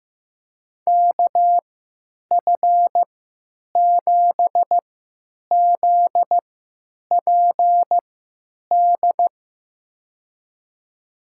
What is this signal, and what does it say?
Morse code, heard as a keyed tone, about 15 wpm, spaced as for 9 wpm, 700 Hz. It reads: KF7ZPD
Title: USA Call Signs - 3x Character Spacing - 15wpm